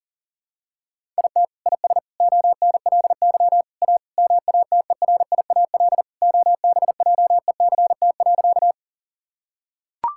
Morse code, heard as a keyed tone, about 40 wpm, 700 Hz. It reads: IT IS ONLY A MATERIAL OBJECT.
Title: Sentences from Top 1000 Words - Review Part 5 - 40wpm